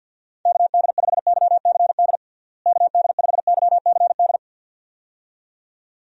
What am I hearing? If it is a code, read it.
Morse code, heard as a keyed tone, about 52 wpm, 700 Hz. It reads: KD5YCD KD5YCD
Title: USA Call Signs - 6 Characters - Repeated 2x - 50wpm